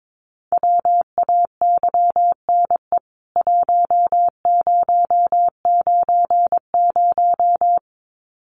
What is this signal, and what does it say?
Morse code, heard as a keyed tone, about 22 wpm, 700 Hz. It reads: WAYNE 1090
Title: CWT Exchange 22wpm